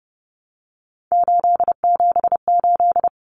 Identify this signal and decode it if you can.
Morse code, heard as a keyed tone, about 30 wpm, 700 Hz. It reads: 878
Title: Common Cut Numbers - 3-Digit 30wpm